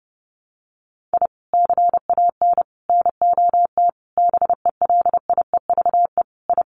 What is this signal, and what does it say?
Morse code, heard as a keyed tone, about 30 wpm, 700 Hz. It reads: I CAN NOT BELIEVE I
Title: Sets of 5 Words Encore2 30wpm